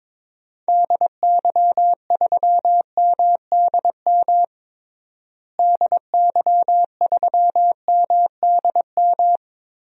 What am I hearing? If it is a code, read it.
Morse code, heard as a keyed tone, about 21 wpm, 700 Hz. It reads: DY3MDM DY3MDM
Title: DXCC-100 Call Signs - 6 Characters - Repeated 2x - 22wpm